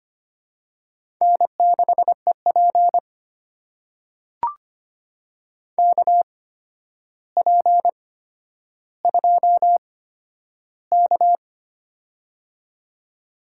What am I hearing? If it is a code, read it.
Morse code, heard as a keyed tone, about 25 wpm, spaced as for 11 wpm, 700 Hz. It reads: N6EP KP2K
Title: USA Call Signs - 4 Characters - 8x Character Spacing - 25wpm